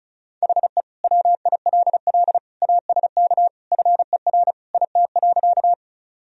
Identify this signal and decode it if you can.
Morse code, heard as a keyed tone, about 35 wpm, 700 Hz. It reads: HE WILL ASK FER IT.
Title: Sentences from Top 200 Words and Abbreviations 35wpm